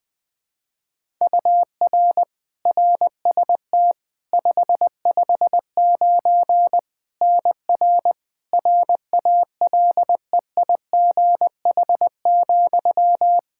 Morse code, heard as a keyed tone, about 20 wpm, 700 Hz. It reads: UR RST 559 NR RALEIGH,
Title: Practice QSO - Exchange 1 - Beginning - 20wpm